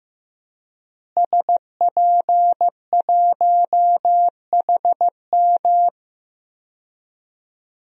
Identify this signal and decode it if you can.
Morse code, heard as a keyed tone, about 15 wpm, 700 Hz. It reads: SP1HM
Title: DXCC-100 Call Signs - 5 Characters - Repeated 2x - 15wpm